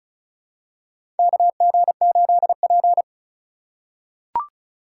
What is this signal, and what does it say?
Morse code, heard as a keyed tone, about 34 wpm, 700 Hz. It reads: KG8P
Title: USA Call Signs - 4 Characters - 4x Character Spacing - 35wpm